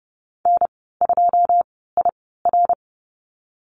Morse code, heard as a keyed tone, about 30 wpm, spaced as for 17 wpm, 700 Hz. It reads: N2IR
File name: Lesson 15g - Number 5 - Call Signs 30wpm